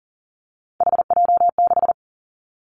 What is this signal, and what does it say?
Morse code, heard as a keyed tone, about 43 wpm, 700 Hz. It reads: HJ6
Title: Callsign Prefixes for Top 100 DXCC - 40wpm